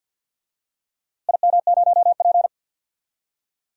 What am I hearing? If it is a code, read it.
Morse code, heard as a keyed tone, about 53 wpm, 700 Hz. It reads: IM0P
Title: DXCC-100 Call Signs - 4 Characters - Repeated 3x - 50wpm